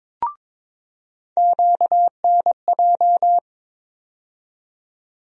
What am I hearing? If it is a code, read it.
Morse code, heard as a keyed tone, about 22 wpm, 700 Hz. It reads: QNJ
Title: USA Call Sign Postfixes - 3 Letters 22wpm